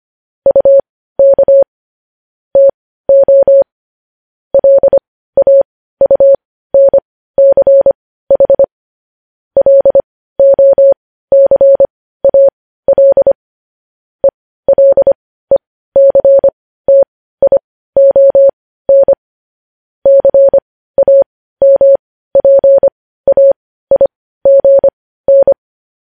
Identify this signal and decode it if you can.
Morse code, heard as a keyed tone, about 25 wpm, spaced as for 15 wpm, 550 Hz. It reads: UK TO LAUNCH LOCAL ELECTION CAMPAIGN
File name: podcast / News headlines-1x-25x15